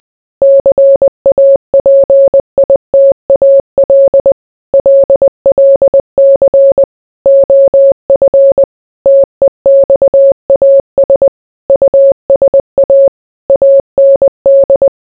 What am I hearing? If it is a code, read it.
Morse code, heard as a keyed tone, about 20 wpm, 550 Hz. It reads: CAPITAL LLC OF TEXAS USA AND